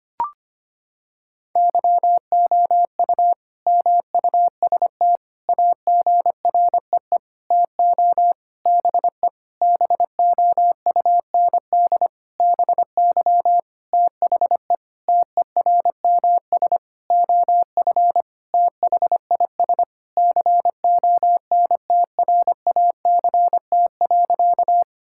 Morse code, heard as a keyed tone, about 25 wpm, 700 Hz. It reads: YOU MUST AGREE TO BE BOUND BY THE TERMS OF THIS CONTRACT.
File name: practice / 25wpm / Sentences with 9-Word Phrases - 25wpm